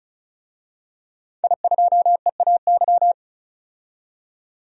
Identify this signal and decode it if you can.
Morse code, heard as a keyed tone, about 36 wpm, 700 Hz. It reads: I2EAY
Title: DXCC-100 Call Signs - 5 Characters - 35wpm